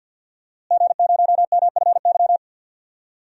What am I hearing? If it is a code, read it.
Morse code, heard as a keyed tone, about 48 wpm, 700 Hz. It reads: G0MFY
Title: DXCC-100 Call Signs - 5 Characters - Repeated 3x - 50wpm